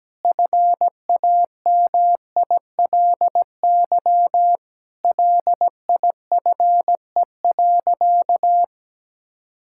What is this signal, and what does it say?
Morse code, heard as a keyed tone, about 17 wpm, 700 Hz. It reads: FAMILY LIFE.
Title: Sentences from Top 1000 Words - Review Part 5 - 17wpm